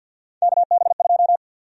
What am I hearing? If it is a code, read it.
Morse code, heard as a keyed tone, about 53 wpm, 700 Hz. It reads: KB2
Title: USA Call Sign Prefixes - 50wpm